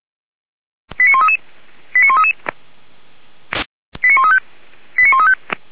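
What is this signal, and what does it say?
Five-tone selective call (ZVEI), heard as quick runs of five tones, tones of about 70 ms. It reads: 89133 89133 89135 89135
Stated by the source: ZVEI five-tone call received by SDR, from the feuerwehr-orlamuende.org fire brigade server.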